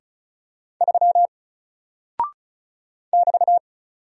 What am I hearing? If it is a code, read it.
Morse code, heard as a keyed tone, about 36 wpm, 700 Hz. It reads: 3 <BT>
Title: Single Character (Letter:Number:Punctuation:Prosign) Alt 35wpm